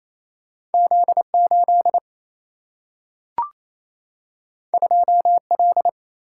Morse code, heard as a keyed tone, about 28 wpm, 700 Hz. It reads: Z8 2L
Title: Single-Letter Single-Number 28wpm